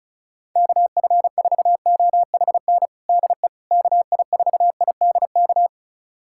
Morse code, heard as a keyed tone, about 36 wpm, 700 Hz. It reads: KF4OHN DE KI4IDK
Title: USA Callsign DE Callsign - 6 Characters - 35wpm